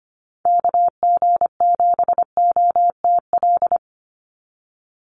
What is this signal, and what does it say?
Morse code, heard as a keyed tone, about 25 wpm, 700 Hz. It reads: KG7OTL
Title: USA Call Signs 25wpm